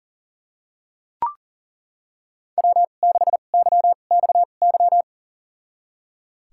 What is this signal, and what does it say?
Morse code, heard as a keyed tone, about 40 wpm, spaced as for 29 wpm, 700 Hz. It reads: W6YXY
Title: USA Call Signs - 2x Character Spacing - 40wpm